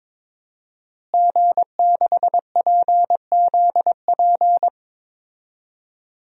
Morse code, heard as a keyed tone, about 23 wpm, 700 Hz. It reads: G6PZP
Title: DXCC-100 Call Signs - 5 Characters - Repeated 2x - 22wpm